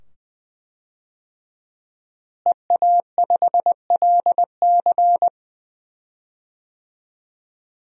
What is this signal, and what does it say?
Morse code, heard as a keyed tone, about 20 wpm, 700 Hz. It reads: EA5LC
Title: DXCC-100 Call Signs - 5 Characters - Repeated 2x - 20wpm